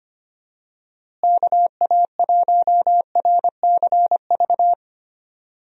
Morse code, heard as a keyed tone, about 25 wpm, 700 Hz. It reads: KA1RCV
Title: USA Call Signs - 6 Characters - Repeated 2x - 25wpm